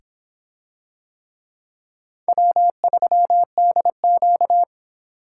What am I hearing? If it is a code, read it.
Morse code, heard as a keyed tone, about 24 wpm, 700 Hz. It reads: W3DQ